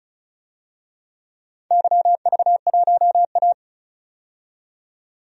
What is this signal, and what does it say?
Morse code, heard as a keyed tone, about 34 wpm, 700 Hz. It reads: YV1A
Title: DXCC-100 Call Signs - 4 Characters - 35wpm